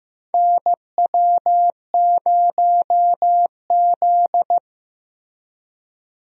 Morse code, heard as a keyed tone, about 15 wpm, 700 Hz. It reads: NW0Z
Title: USA Call Signs - 4 Characters - Repeated 3x - 15wpm